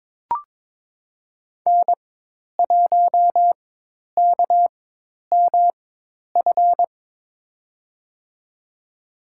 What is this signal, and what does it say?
Morse code, heard as a keyed tone, about 22 wpm, spaced as for 10 wpm, 700 Hz. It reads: N1KMF
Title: USA Call Signs - 5 Characters - 4x Character Spacing - 22wpm